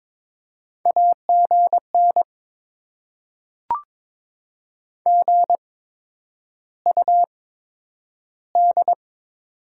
Morse code, heard as a keyed tone, about 22 wpm, spaced as for 9 wpm, 700 Hz. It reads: AGN GUD
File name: Top 20 QSO Elements - 8x Character Spacing - 22wpm